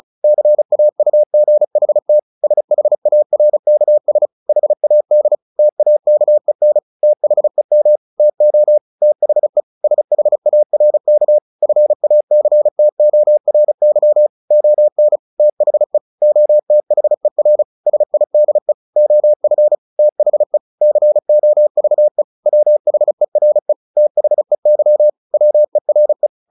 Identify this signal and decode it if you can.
Morse code, heard as a keyed tone, about 34 wpm, 600 Hz. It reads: CAUGHT SHARKS HAD TAKEN THEM TO THE SHARK FACTORY ON THE OTHER SIDE OF THE COVE WHERE THEY WERE
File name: the old man and the sea / the old man and the sea 1 35wpm